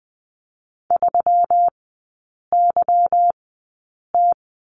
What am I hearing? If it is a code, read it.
Morse code, heard as a keyed tone, about 20 wpm, spaced as for 8 wpm, 700 Hz. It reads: 3YT
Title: 178 Lesson 29e - Slash - Three Characters 20wpm